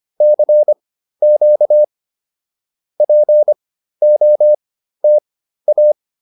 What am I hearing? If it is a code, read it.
Morse code, heard as a keyed tone, about 25 wpm, spaced as for 13 wpm, 600 Hz. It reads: CQ POTA